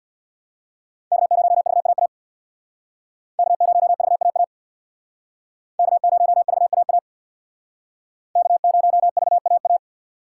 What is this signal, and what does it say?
Morse code, heard as a keyed tone, about 65 wpm, 690 Hz. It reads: K0VAA K0VAA K0VAA K0VAA